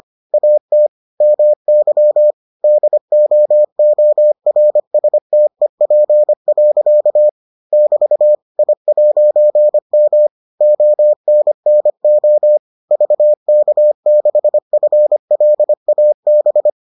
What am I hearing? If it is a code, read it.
Morse code, heard as a keyed tone, about 25 wpm, 600 Hz. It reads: AT MY DOORSTEP. <BT> I'M ONNO VK6FLAB